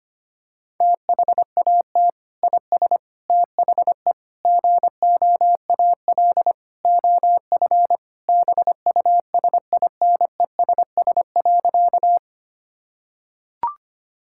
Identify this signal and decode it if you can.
Morse code, heard as a keyed tone, about 25 wpm, 700 Hz. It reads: THAT IS THE GOAL OF BUSINESS.
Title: Sentences from Top 1000 Words - Review Part 5 - 25wpm